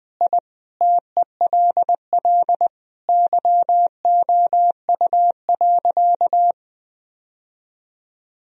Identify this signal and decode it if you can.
Morse code, heard as a keyed tone, about 20 wpm, 700 Hz. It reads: I TELL YOU.
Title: Sentences from Top 1000 Words - Review Part 3 - 20wpm